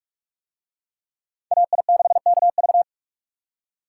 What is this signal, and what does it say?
Morse code, heard as a keyed tone, about 43 wpm, 700 Hz. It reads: AI6KV